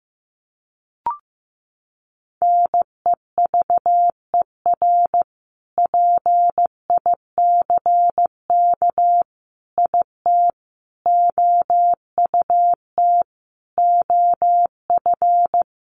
Morse code, heard as a keyed tone, about 15 wpm, 700 Hz. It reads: NEVER PICK IT OUT OF
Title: Sentences from Top 1000 Words 15wpm